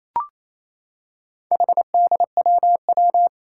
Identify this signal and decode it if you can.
Morse code, heard as a keyed tone, about 29 wpm, 700 Hz. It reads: HDWW
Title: Four Challenging Letter-Number Combinations 28wpm